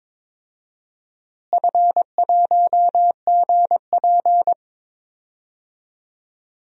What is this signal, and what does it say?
Morse code, heard as a keyed tone, about 22 wpm, 700 Hz. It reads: F1GP
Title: DXCC-100 Call Signs - 4 Characters - Repeated 3x - 22wpm